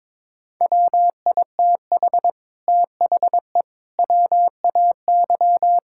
Morse code, heard as a keyed tone, about 22 wpm, 700 Hz. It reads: WITH THE WAY
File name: practice / 22wpm / Sets of 8 Words - 22wpm